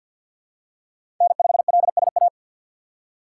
Morse code, heard as a keyed tone, about 47 wpm, 700 Hz. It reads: N5LSA
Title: USA Call Signs - 5 Characters - 4x Character Spacing - 50wpm